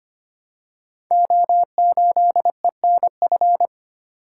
Morse code, heard as a keyed tone, about 25 wpm, 700 Hz. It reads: O8ENF
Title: Code Groups - 5 Letter:Numbers - 25wpm